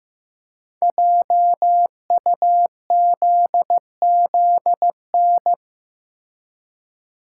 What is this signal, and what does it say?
Morse code, heard as a keyed tone, about 15 wpm, 700 Hz. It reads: JUZZN
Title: Code Groups - 5 Letters - 15wpm